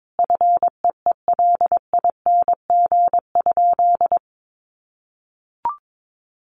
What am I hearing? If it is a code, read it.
Morse code, heard as a keyed tone, about 22 wpm, 700 Hz. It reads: FEELING?
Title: Sentences from Top 1200 Words 22wpm